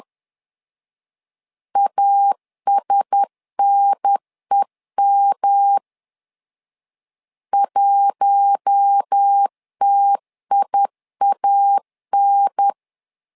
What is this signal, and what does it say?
Morse code, heard as a keyed tone, about 10 wpm, 800 Hz. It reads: ASNEM 1TIAN
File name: LEZIONI / 4 Q1S 10wpm 2133